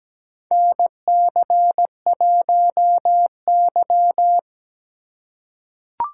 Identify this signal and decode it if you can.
Morse code, heard as a keyed tone, about 17 wpm, 700 Hz. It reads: NC1Y
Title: USA Call Signs - 4 Characters - 2x Character Spacing - 17wpm